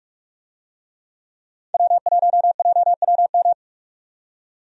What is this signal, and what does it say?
Morse code, heard as a keyed tone, about 46 wpm, 700 Hz. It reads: W1JWM
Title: USA Call Signs - 5 Characters - 45wpm